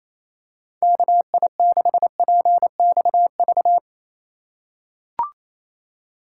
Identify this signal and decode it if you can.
Morse code, heard as a keyed tone, about 28 wpm, 700 Hz. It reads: KI6PXV
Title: USA Call Signs - 6 Characters - 2x Character Spacing - 28wpm